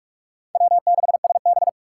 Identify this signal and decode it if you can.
Morse code, heard as a keyed tone, about 44 wpm, 700 Hz. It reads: W6SB
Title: USA Call Signs - 4 Characters - 6x Character Spacing - 45wpm